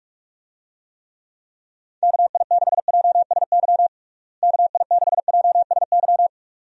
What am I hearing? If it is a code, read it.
Morse code, heard as a keyed tone, about 46 wpm, 700 Hz. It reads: KI6JSY KI6JSY
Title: USA Call Signs - 6 Characters - Repeated 2x - 45wpm